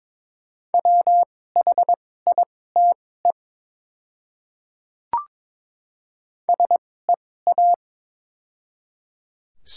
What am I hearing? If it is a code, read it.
Morse code, heard as a keyed tone, about 22 wpm, spaced as for 16 wpm, 700 Hz. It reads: WHITE SEA
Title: 2x Character Spacing - Top 300 Words - 22wpm